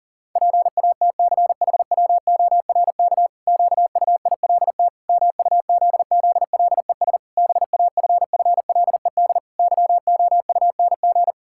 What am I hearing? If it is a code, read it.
Morse code, heard as a keyed tone, about 40 wpm, 700 Hz. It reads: PATCHWORK QUILT MUZZLES BAFFLED YOUNG
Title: Pangram Challenge v2 - 40wpm